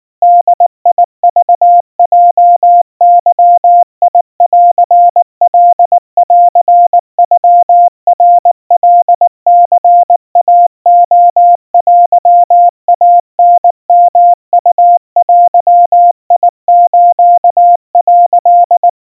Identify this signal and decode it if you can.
Morse code (Japanese Wabun code), heard as a keyed tone, about 19 wpm, 700 Hz. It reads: ボクヲゲンカンノナカニイレテイタヨウデス」
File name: kawa / Morse sample / kamyu 70 cpm 700 Hz Crispy